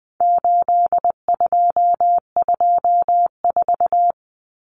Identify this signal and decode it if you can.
Morse code, heard as a keyed tone, about 20 wpm, 700 Hz. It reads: 8224
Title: Common Cut Numbers - 4-Digit 20wpm